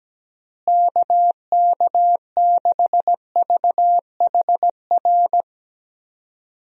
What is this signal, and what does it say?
Morse code, heard as a keyed tone, about 17 wpm, 700 Hz. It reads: KK6VHR
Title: USA Call Signs - 6 Characters - Repeated 3x - 17wpm